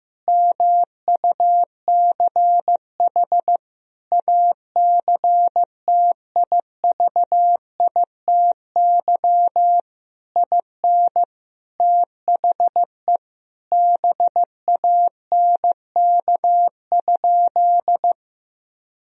Morse code, heard as a keyed tone, about 15 wpm, 700 Hz. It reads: MUCH ACTIVITY IN THE BANK?
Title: Sentences from Top 1000 Words - Review Part 4 - 15wpm